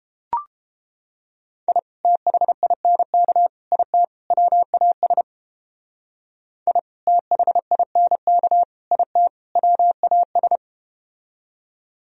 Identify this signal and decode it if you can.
Morse code, heard as a keyed tone, about 32 wpm, 700 Hz. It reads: I THINK IT WAS I THINK IT WAS